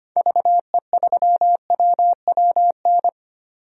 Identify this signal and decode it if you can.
Morse code, heard as a keyed tone, about 25 wpm, 700 Hz. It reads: VE3WWN
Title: DXCC-100 Call Signs - 6 Characters - 25wpm